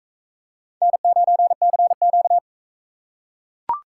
Morse code, heard as a keyed tone, about 40 wpm, 700 Hz. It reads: N9CQ